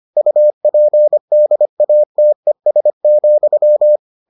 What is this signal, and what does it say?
Morse code, heard as a keyed tone, about 25 wpm, 600 Hz. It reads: UPDATES,